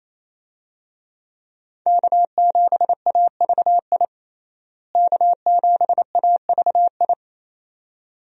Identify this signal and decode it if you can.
Morse code, heard as a keyed tone, about 28 wpm, 700 Hz. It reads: K7AVI K7AVI
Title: USA Call Signs - 5 Characters - Repeated 2x - 28wpm